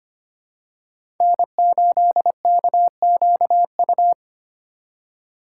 Morse code, heard as a keyed tone, about 25 wpm, 700 Hz. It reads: N8KQU
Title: USA Call Signs - 5 Characters - 4x Character Spacing - 25wpm